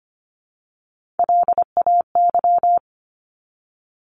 Morse code, heard as a keyed tone, about 25 wpm, 700 Hz. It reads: LAY
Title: Top 500 Words 25wpm